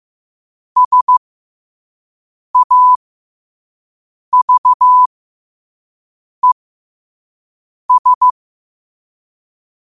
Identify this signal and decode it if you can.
Morse code, heard as a keyed tone, about 15 wpm, spaced as for 5 wpm, 1000 Hz. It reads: SAVES
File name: qotdfiles / 5wpm